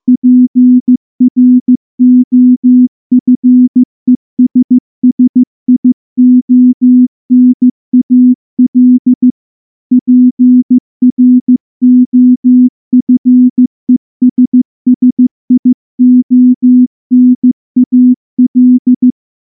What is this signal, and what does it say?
Morse code, heard as a keyed tone, about 14 wpm, 250 Hz. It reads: PROFESSIONAL PROFESSIONAL
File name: morse / morse-training-15WPM